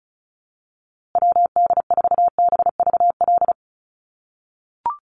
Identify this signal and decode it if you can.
Morse code, heard as a keyed tone, about 34 wpm, 700 Hz. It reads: WD4BVL
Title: USA Call Signs Repeated 2x 35wpm